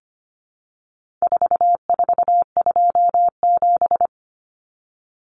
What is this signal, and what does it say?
Morse code, heard as a keyed tone, about 25 wpm, 700 Hz. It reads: 4427
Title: Numbers - 4-Digit 25wpm